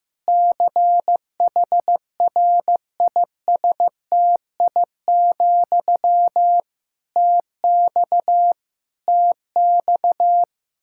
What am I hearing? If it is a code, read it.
Morse code, heard as a keyed tone, about 16 wpm, 700 Hz. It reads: CHRISTI, TX TX
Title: K1USN SST QSO Practice 15wpm